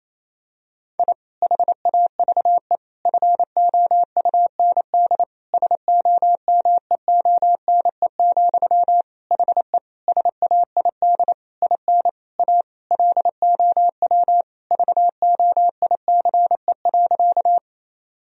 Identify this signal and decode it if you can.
Morse code, heard as a keyed tone, about 28 wpm, 700 Hz. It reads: I HAVE FOUND SOMEONE, HE SAID IN A LOW VOICE.
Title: Sentences from Top 1000 Words - Review Part 1 - 28wpm